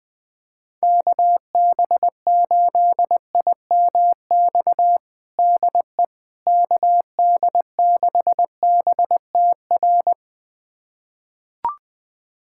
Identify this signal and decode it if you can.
Morse code, heard as a keyed tone, about 20 wpm, 700 Hz. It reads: KB8IMX DE KD6BTR
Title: USA Callsign DE Callsign 20wpm